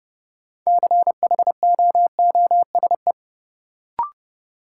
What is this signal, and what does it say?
Morse code, heard as a keyed tone, about 31 wpm, 700 Hz. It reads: CHOOSE
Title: Sentences from Top 700 Words - New Word Drill - 30wpm